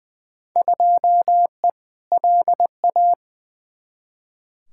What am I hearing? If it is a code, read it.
Morse code, heard as a keyed tone, about 20 wpm, 700 Hz. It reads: 2E LA
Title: Field Day Exchanges Encore - 20wpm